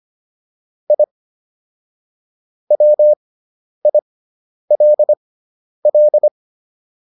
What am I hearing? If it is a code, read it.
Morse code, heard as a keyed tone, about 25 wpm, spaced as for 10 wpm, 600 Hz. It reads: I WILL